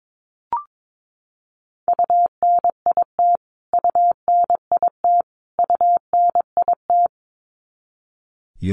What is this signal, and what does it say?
Morse code, heard as a keyed tone, about 22 wpm, 700 Hz. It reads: UNIT UNIT UNIT
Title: Top 500 Words Repeated 22wpm